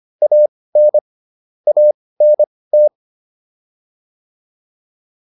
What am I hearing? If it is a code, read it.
Morse code, heard as a keyed tone, about 25 wpm, spaced as for 18 wpm, 600 Hz. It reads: AN ANT